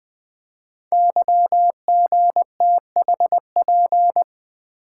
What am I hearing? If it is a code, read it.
Morse code, heard as a keyed tone, about 20 wpm, 700 Hz. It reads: YGTHP
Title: Code Groups - 5 Letters - 20wpm